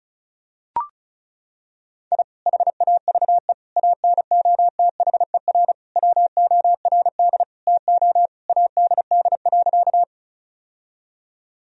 Morse code, heard as a keyed tone, about 36 wpm, 700 Hz. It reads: I HAVE ANOTHER WORD TO ADD.
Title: Sentences from Top 1000 Words - Review Part 1 - 35wpm